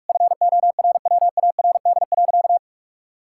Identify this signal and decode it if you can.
Morse code, heard as a keyed tone, about 41 wpm, 700 Hz. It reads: FORWARD.
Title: Pangram Challenge v2 - 45wpm